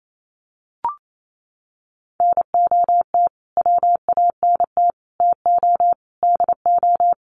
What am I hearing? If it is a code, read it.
Morse code, heard as a keyed tone, about 28 wpm, 700 Hz. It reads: NOT WANT TO DO
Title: Sets of 5 Words Encore 28wpm